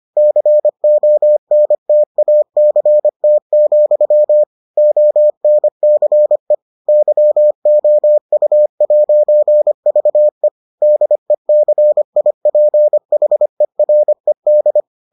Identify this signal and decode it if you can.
Morse code, heard as a keyed tone, about 25 wpm, 600 Hz. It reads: CONTACT, ONCE YOU'VE DECIPHERED